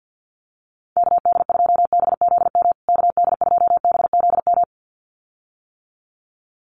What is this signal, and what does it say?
Morse code, heard as a keyed tone, about 50 wpm, 700 Hz. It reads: KD2BZM KD2BZM
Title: USA Call Signs Repeated 2x 50wpm